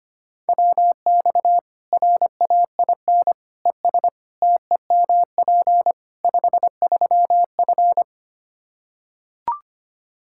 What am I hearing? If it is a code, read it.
Morse code, heard as a keyed tone, about 25 wpm, 700 Hz. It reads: WX RAIN ES TEMP 53F
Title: Practice QSO - Exchange 2 - Info - 25wpm